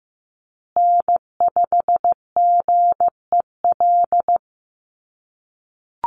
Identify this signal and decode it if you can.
Morse code, heard as a keyed tone, about 15 wpm, 700 Hz. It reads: N5GEL
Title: USA Call Signs Repeated 2x 15wpm